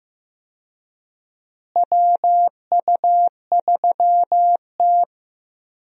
Morse code, heard as a keyed tone, about 15 wpm, 700 Hz. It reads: WU3T
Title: USA Call Signs - 4 Characters - Repeated 2x - 15wpm